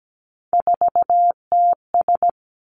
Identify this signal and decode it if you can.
Morse code, heard as a keyed tone, about 17 wpm, 700 Hz. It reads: 4TS
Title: Three Challenging Letter-Number Combinations 17wpm